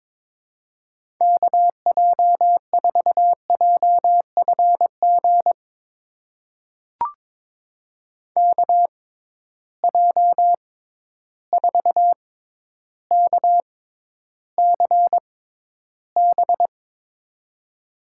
Morse code, heard as a keyed tone, about 22 wpm, spaced as for 11 wpm, 700 Hz. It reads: KJ4JFG KJ4KCB
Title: USA Call Signs - 6 Characters - 6x Character Spacing - 22wpm